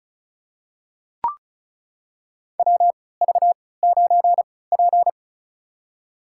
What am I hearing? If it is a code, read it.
Morse code, heard as a keyed tone, about 36 wpm, spaced as for 20 wpm, 700 Hz. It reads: WV9P
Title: USA Call Signs - 4 Characters - 3x Character Spacing - 35wpm